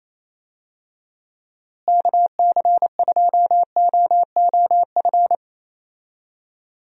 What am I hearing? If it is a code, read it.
Morse code, heard as a keyed tone, about 28 wpm, 700 Hz. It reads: KC2OOF
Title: USA Call Signs - 6 Characters - Repeated 3x - 28wpm